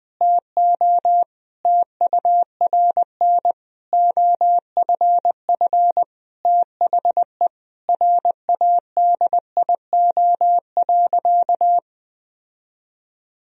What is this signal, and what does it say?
Morse code, heard as a keyed tone, about 20 wpm, 700 Hz. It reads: TO TURN OFF THE RADIO.
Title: Sentences from Top 1000 Words - Review Part 3 - 20wpm